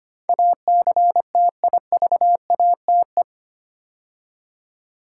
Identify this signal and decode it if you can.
Morse code, heard as a keyed tone, about 25 wpm, 700 Hz. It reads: ACTIVATE
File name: 8 Letter Words - First 3 Letters Given - 25wpm